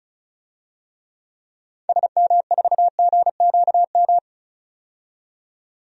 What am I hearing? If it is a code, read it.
Morse code, heard as a keyed tone, about 36 wpm, 700 Hz. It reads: SM4GQM
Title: DXCC-100 Call Signs - 6 Characters - 35wpm